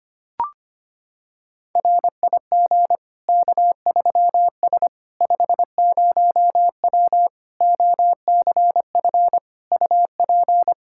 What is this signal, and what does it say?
Morse code, heard as a keyed tone, about 25 wpm, 700 Hz. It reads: RIG K3S 50W OCF UP
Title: K1USN SST QSO Practice 25wpm